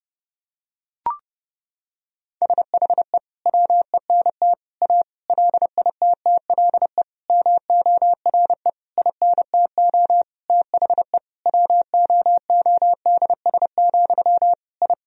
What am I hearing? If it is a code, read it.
Morse code, heard as a keyed tone, about 30 wpm, 700 Hz. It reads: SHE WENT A LITTLE MORE INTO THE WOODS, I
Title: Sentences from Top 1000 Words - Review Part 2 - 30wpm